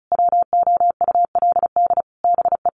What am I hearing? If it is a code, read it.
Morse code, heard as a keyed tone, about 34 wpm, 700 Hz. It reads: WOULD BE
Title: Sentences from Top 1400 Words 35wpm